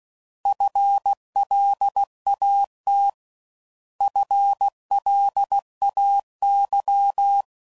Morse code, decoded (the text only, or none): FLAT FLAY